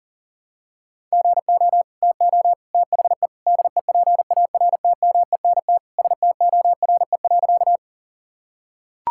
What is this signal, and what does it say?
Morse code, heard as a keyed tone, about 40 wpm, 700 Hz. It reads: GO TO THE DEPARTMENT STORE.
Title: Sentences from Top 1000 Words - Review Part 5 - 40wpm